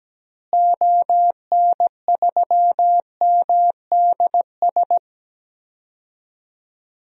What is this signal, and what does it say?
Morse code, heard as a keyed tone, about 17 wpm, 700 Hz. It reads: ON3MDS
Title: DXCC-100 Call Signs - 6 Characters - Repeated 2x - 17wpm